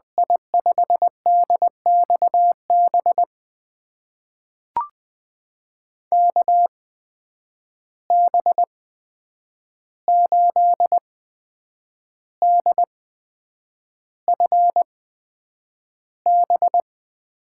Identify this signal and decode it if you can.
Morse code, heard as a keyed tone, about 20 wpm, spaced as for 8 wpm, 700 Hz. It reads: I5DXB KB8DFB
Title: USA Call Signs - 6 Characters - 8x Character Spacing - 20wpm